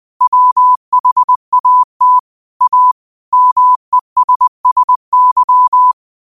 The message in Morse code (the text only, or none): WHAT A MESSY